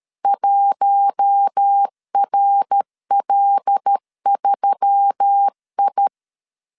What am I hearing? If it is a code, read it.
Morse code, heard as a keyed tone, about 13 wpm, 800 Hz. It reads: 1RL3I
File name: CW / DET / 1 6 1500C S A 12wpm 3440